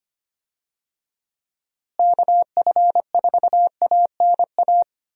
Morse code, heard as a keyed tone, about 25 wpm, 700 Hz. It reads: KF4ANA
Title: USA Call Signs - 6 Characters - Repeated 3x - 25wpm